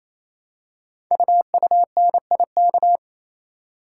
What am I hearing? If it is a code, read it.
Morse code, heard as a keyed tone, about 29 wpm, 700 Hz. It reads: UUNIK
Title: Code Groups - 5 Letters - 28wpm